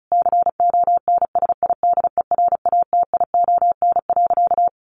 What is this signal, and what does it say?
Morse code, heard as a keyed tone, about 34 wpm, 700 Hz. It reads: CONSIDERATION.
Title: Sentences from Top 1500 Words 35wpm